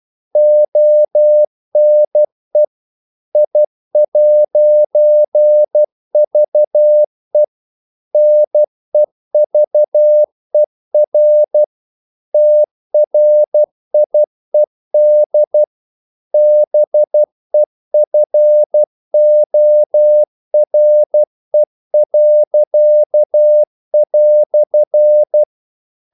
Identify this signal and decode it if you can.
Morse code, heard as a keyed tone, about 12 wpm, 600 Hz. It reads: ONE I'VE NEVER TRIED BEFORE."
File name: CW / Practice / EnglishQuotations0312